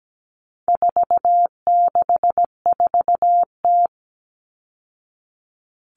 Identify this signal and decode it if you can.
Morse code, heard as a keyed tone, about 17 wpm, 700 Hz. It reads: 464T
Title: All Cut Numbers - 4-Digit 17wpm